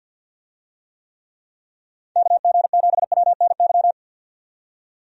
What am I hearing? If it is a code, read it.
Morse code, heard as a keyed tone, about 53 wpm, 700 Hz. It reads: KG7WNY
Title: USA Call Signs - 6 Characters - 50wpm